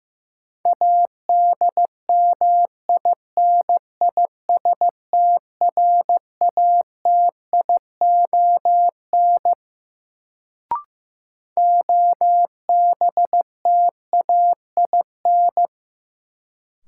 Morse code, heard as a keyed tone, about 16 wpm, 700 Hz. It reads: ADMINISTRATION OBTAIN
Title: Sentences from Top 1300 Words - New Word Drill - 15wpm